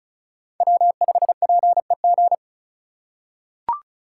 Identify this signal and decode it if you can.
Morse code, heard as a keyed tone, about 36 wpm, 700 Hz. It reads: W5PEG
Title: USA Call Signs - 5 Characters - 3x Character Spacing - 35wpm